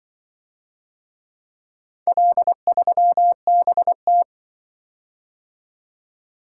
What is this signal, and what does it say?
Morse code, heard as a keyed tone, about 24 wpm, 700 Hz. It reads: L3BT
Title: Speed-Racing - Four Letter-Number Combinations - 20wpm